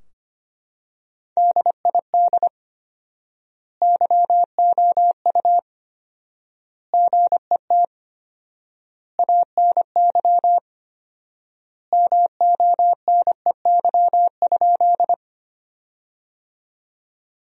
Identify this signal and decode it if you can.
Morse code, heard as a keyed tone, about 25 wpm, 700 Hz. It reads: DID YOU GET ANY MONEY?
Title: 4x Word-Spacing - Sentences from Top 400 Words - 25wpm